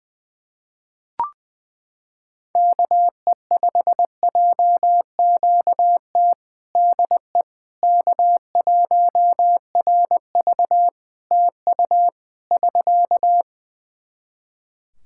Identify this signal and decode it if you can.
Morse code, heard as a keyed tone, about 20 wpm, 700 Hz. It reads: KE5JQT DE K1RV TU <SK>
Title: Practice QSO - Exchange 4 - Ending - 20wpm